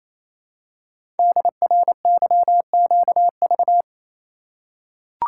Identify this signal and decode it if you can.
Morse code, heard as a keyed tone, about 29 wpm, 700 Hz. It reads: DRYQV
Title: Code Groups - 5 Letters - 28wpm